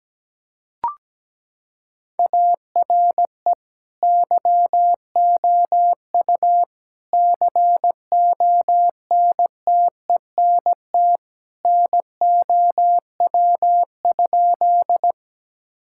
Morse code, heard as a keyed tone, about 17 wpm, 700 Hz. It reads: ARE YOU CONTENT NOW?
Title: Sentences from Top 1000 Words - Review Part 6 - 17wpm